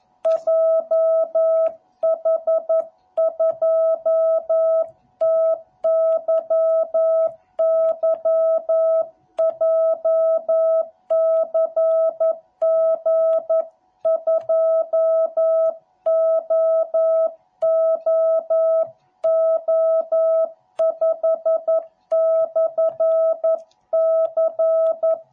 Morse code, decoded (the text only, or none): JH2TYY JCG2OOO5/C